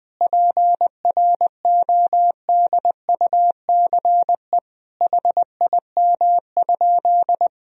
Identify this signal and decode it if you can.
Morse code, heard as a keyed tone, about 20 wpm, 700 Hz. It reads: PRODUCE HIM?
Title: Sentences from Top 1000 Words - Review Part 3 - 20wpm